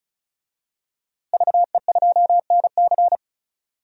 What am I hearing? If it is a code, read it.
Morse code, heard as a keyed tone, about 36 wpm, 700 Hz. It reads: VE2NC
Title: DXCC-100 Call Signs - 5 Characters - Repeated 2x - 35wpm